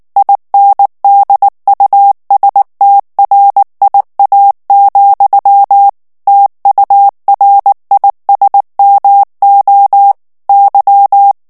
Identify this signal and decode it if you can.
Morse code, heard as a keyed tone, about 19 wpm, 800 Hz. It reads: INDUSTRIA, TURISMO Y
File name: morse mp3 / ESPAGNOL TEXTE EN CLAIR 2 VITESSE 18 MOTS MINUTES